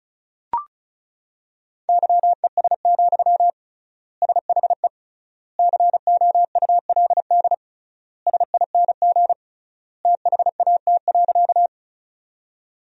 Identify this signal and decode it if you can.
Morse code, heard as a keyed tone, about 36 wpm, 700 Hz. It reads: YES, SHE COULD SING THAT.
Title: 3x Word-Spacing - Sentences from Top 500 Words - 35wpm